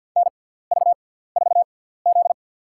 Morse code, heard as a keyed tone, about 53 wpm, spaced as for 17 wpm, 690 Hz. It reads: NV4Z
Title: USA Call Signs - 4 Characters - 6x Character Spacing - 50wpm